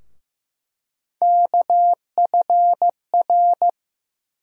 Morse code, heard as a keyed tone, about 15 wpm, 700 Hz. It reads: KFR
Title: USA Call Sign Postfixes - 3 Letters 15wpm